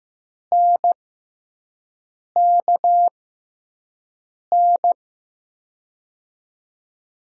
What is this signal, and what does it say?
Morse code, heard as a keyed tone, about 15 wpm, spaced as for 5 wpm, 700 Hz. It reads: NKN
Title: USA Call Signs - 6 Characters - 6x Character Spacing - 15wpm